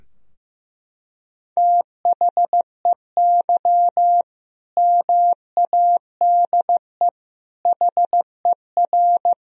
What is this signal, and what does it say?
Morse code, heard as a keyed tone, about 15 wpm, 700 Hz. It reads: THEY MADE HER
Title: Sentences from Top 1000 Words - Review Part 2 - 15wpm